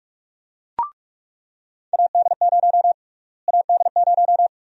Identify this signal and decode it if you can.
Morse code, heard as a keyed tone, about 45 wpm, 700 Hz. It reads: AD0 AD0
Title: USA Call Sign Prefixes - Repeated 2x - 45wpm